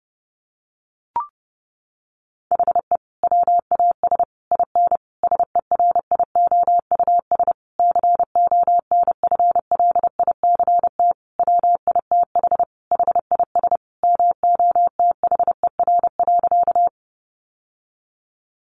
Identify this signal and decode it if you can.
Morse code, heard as a keyed tone, about 30 wpm, 700 Hz. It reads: HE WAS IN SERIOUS CONFLICT WITH HIS MOTHER.
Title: Sentences from Top 1300 Words 30wpm